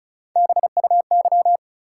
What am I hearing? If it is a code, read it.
Morse code, heard as a keyed tone, about 36 wpm, 700 Hz. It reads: BUY
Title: Sentences from Top 700 Words - New Word Drill - 35wpm